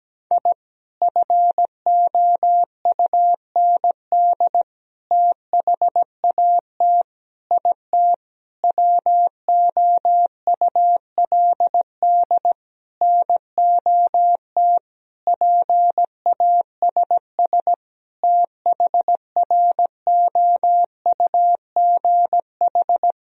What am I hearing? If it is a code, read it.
Morse code, heard as a keyed tone, about 17 wpm, 700 Hz. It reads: I FOUND THAT IT WOULD NOT PASS THROUGH
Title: Sentences from Top 1000 Words - Review Part 1 - 17wpm